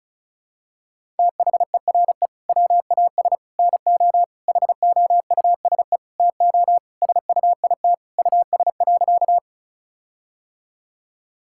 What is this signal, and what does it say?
Morse code, heard as a keyed tone, about 36 wpm, 700 Hz. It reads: THERE WAS NO HOUSE TO SUIT US.
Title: Sentences from Top 1600 Words 35wpm